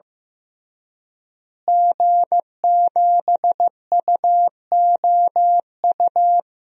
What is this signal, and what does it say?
Morse code, heard as a keyed tone, about 15 wpm, 700 Hz. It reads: G7UOU
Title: DXCC-100 Call Signs - 5 Characters - Repeated 3x - 15wpm